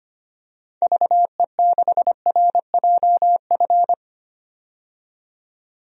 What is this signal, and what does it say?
Morse code, heard as a keyed tone, about 25 wpm, 700 Hz. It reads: VE6RJF
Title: DXCC-100 Call Signs - 6 Characters - 25wpm